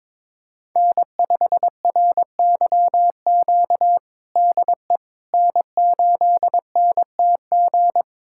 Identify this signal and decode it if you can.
Morse code, heard as a keyed tone, about 22 wpm, 700 Hz. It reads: N5RYQ DE N8NTG
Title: USA Callsign DE Callsign - 5 Characters - 22wpm